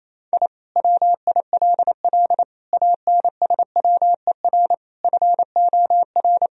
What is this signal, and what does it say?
Morse code, heard as a keyed tone, about 28 wpm, 700 Hz. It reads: I WILL ANSWER FOR
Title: Sentences from Top 1000 Words - Review Part 4 - 28wpm